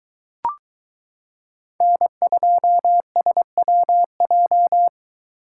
Morse code, heard as a keyed tone, about 22 wpm, 700 Hz. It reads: N2SWJ